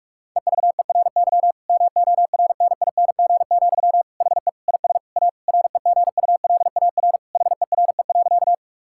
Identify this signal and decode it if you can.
Morse code, heard as a keyed tone, about 46 wpm, 700 Hz. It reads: EVERY MORNING, HE IS A REGULAR HERE.
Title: Sentences from Top 1000 Words - Review Part 6 - 45wpm